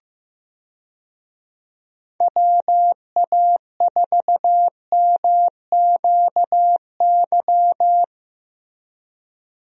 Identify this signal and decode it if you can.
Morse code, heard as a keyed tone, about 15 wpm, 700 Hz. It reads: WA4MQY
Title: USA Call Signs - 6 Characters - Repeated 3x - 15wpm